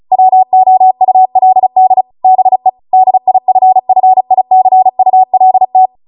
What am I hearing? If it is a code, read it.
Morse code, heard as a keyed tone, about 36 wpm, 750 Hz. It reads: WOULD BE DIFFICULT
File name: Morse / 35WPM